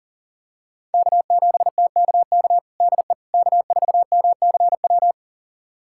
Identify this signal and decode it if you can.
Morse code, heard as a keyed tone, about 40 wpm, 700 Hz. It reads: K7TKK DE K4MCW
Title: USA Callsign DE Callsign - 5 Characters - 40wpm